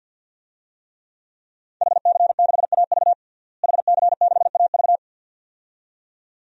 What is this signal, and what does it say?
Morse code, heard as a keyed tone, about 48 wpm, 700 Hz. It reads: HC6AV HC6AV
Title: DXCC-100 Call Signs - 5 Characters - Repeated 2x - 50wpm